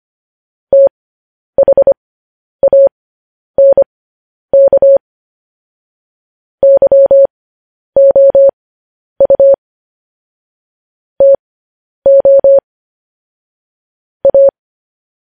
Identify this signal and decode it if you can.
Morse code, heard as a keyed tone, about 25 wpm, spaced as for 10 wpm, 550 Hz. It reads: THANK YOU TO A
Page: News Headlines in Morse Code at 10 WPM